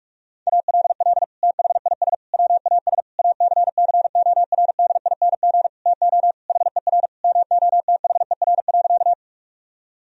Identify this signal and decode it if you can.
Morse code, heard as a keyed tone, about 46 wpm, 700 Hz. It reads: ALL THIS WAS ACCORDING TO HER MOTHER.
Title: Sentences from Top 1000 Words - Review Part 4 - 45wpm